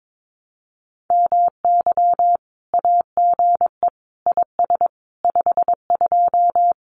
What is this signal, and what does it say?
Morse code, heard as a keyed tone, about 22 wpm, 700 Hz. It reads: MY AGE IS 52
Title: Common QSO Phrases Encore 22wpm